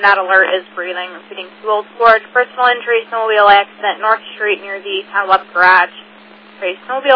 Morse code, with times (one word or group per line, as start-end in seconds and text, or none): none